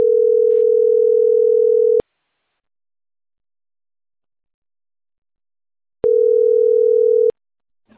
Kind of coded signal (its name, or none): telephone call progress tone